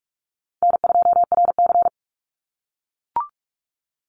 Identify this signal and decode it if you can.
Morse code, heard as a keyed tone, about 44 wpm, 700 Hz. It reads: N2RC
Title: Easy Call Signs - 45wpm